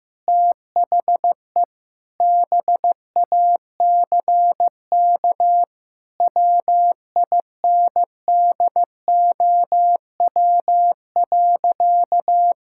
Morse code, heard as a keyed tone, about 16 wpm, 700 Hz. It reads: THE BACK WINDOW.
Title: Sentences from Top 1600 Words 15wpm